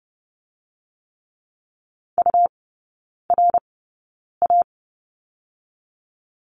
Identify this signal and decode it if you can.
Morse code, heard as a keyed tone, about 30 wpm, spaced as for 9 wpm, 700 Hz. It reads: URA
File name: Lesson 17e - Letter u - Three Characters 30wpm